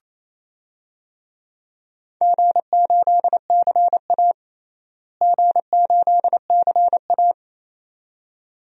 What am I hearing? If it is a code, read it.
Morse code, heard as a keyed tone, about 28 wpm, 700 Hz. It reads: G8CA G8CA
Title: DXCC-100 Call Signs - 4 Characters - Repeated 2x - 28wpm